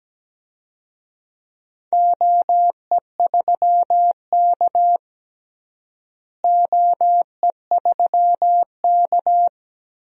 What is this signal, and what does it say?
Morse code, heard as a keyed tone, about 17 wpm, 700 Hz. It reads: OE3K OE3K
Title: DXCC-100 Call Signs - 4 Characters - Repeated 3x - 17wpm